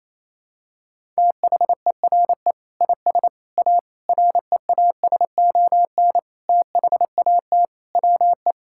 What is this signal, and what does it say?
Morse code, heard as a keyed tone, about 28 wpm, 700 Hz. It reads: THERE IS A REASON THAT WE
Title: Sentences from Top 1000 Words - Review Part 5 - 28wpm